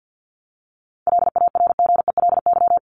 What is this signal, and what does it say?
Morse code, heard as a keyed tone, about 50 wpm, 700 Hz. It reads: LARGELY
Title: LY Suffix - Top 100 Words with Base Words 50wpm